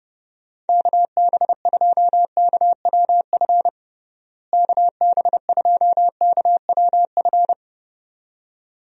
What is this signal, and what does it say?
Morse code, heard as a keyed tone, about 32 wpm, 700 Hz. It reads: KB2KWF KB2KWF
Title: USA Call Signs - 6 Characters - Repeated 2x - 30wpm